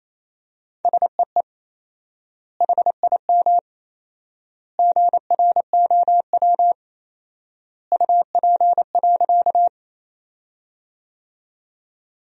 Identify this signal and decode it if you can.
Morse code, heard as a keyed tone, about 27 wpm, 700 Hz. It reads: SEE HIM GROW UP.
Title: 4x Word-Spacing - Sentences from Top 300 Words - 28wpm